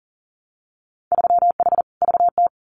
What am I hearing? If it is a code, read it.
Morse code, heard as a keyed tone, about 40 wpm, 700 Hz. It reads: 3H VT
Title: Winter Field Day Exchanges Encore 40wpm